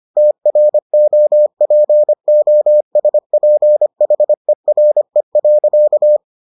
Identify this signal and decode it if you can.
Morse code, heard as a keyed tone, about 25 wpm, 600 Hz. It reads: TROPOSPHERE.